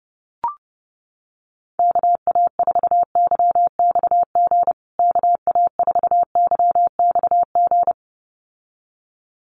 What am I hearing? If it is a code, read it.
Morse code, heard as a keyed tone, about 29 wpm, 700 Hz. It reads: KA4YXG KA4YXG
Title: USA Call Signs Repeated 2x 30wpm